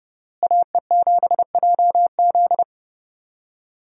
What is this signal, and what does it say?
Morse code, heard as a keyed tone, about 30 wpm, 700 Hz. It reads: AE7JZ
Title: USA Call Signs - 5 Characters - Repeated 3x - 30wpm